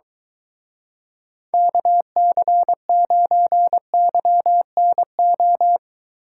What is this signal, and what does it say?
Morse code, heard as a keyed tone, about 23 wpm, 700 Hz. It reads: KC9YNO